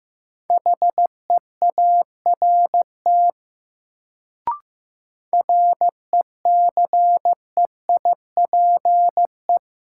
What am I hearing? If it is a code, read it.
Morse code, heard as a keyed tone, about 15 wpm, 700 Hz. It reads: HEART RECEIPE
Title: Sentences from Top 1200 Words - New Word Drill - 15wpm